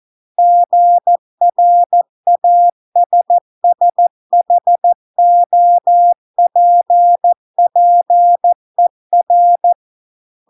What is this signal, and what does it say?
Morse code, heard as a keyed tone, about 14 wpm, 700 Hz. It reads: GRASSHOPPER